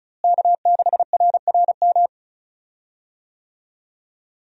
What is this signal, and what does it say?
Morse code, heard as a keyed tone, about 36 wpm, 700 Hz. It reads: K6RRM
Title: USA Call Signs - 5 Characters - Repeated 2x - 35wpm